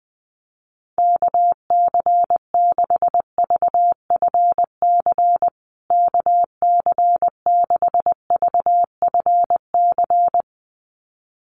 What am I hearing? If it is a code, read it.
Morse code, heard as a keyed tone, about 20 wpm, 700 Hz. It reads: KC6VFC KC6VFC
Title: USA Call Signs Repeated 2x 20wpm